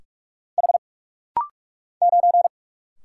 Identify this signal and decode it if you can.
Morse code, heard as a keyed tone, about 45 wpm, 700 Hz. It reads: H 9
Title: Single Letter-Number Alt - Rapid-Fire 45wpm